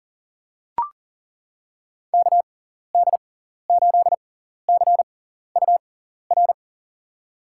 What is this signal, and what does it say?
Morse code, heard as a keyed tone, about 40 wpm, spaced as for 14 wpm, 700 Hz. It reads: KD8CUR
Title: USA Call Signs - 6 Characters - 6x Character Spacing - 40wpm